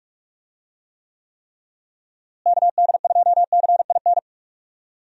Morse code, heard as a keyed tone, about 42 wpm, 700 Hz. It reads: KD2CIN
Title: USA Call Signs - 6 Characters - Repeated 3x - 45wpm